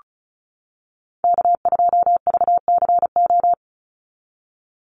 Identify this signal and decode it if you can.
Morse code, heard as a keyed tone, about 34 wpm, 700 Hz. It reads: K2VCO
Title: Call Signs - 35wpm